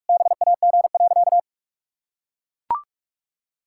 Morse code, heard as a keyed tone, about 41 wpm, 700 Hz. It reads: BAG.
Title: Pangram Challenge v2 - 45wpm